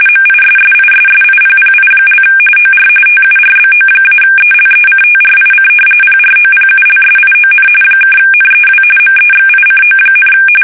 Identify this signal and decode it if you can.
RTTY, about 45.45 baud. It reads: MISSISSIPPI) OR TURBO-ELECTRIC PROPULSION (NEW MEXICO,
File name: fsk850 / battleships